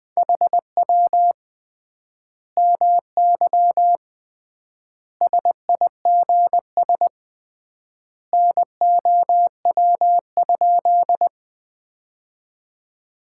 Morse code, heard as a keyed tone, about 20 wpm, 700 Hz. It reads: HW MY SIGS NOW?
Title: Common QSO Phrases - 3x Word-Spacing - 20wpm